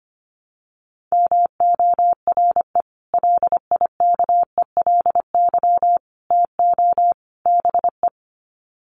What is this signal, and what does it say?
Morse code, heard as a keyed tone, about 25 wpm, 700 Hz. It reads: MORE LIKELY TO BE
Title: Sets of 4 Words 25wpm